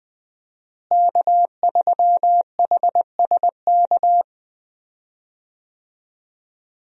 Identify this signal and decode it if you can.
Morse code, heard as a keyed tone, about 20 wpm, 700 Hz. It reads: K3HSK
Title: USA Call Signs - 5 Characters - Repeated 2x - 20wpm